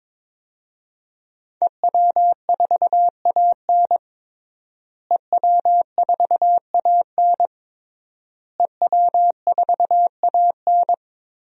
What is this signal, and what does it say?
Morse code, heard as a keyed tone, about 21 wpm, 700 Hz. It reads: EW4AN EW4AN EW4AN
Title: DXCC-100 Call Signs - 5 Characters - Repeated 3x - 22wpm